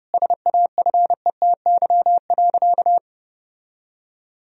Morse code, heard as a keyed tone, about 32 wpm, 700 Hz. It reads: SAFETY.
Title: Sentences from Top 1000 Words - Review Part 4 - 30wpm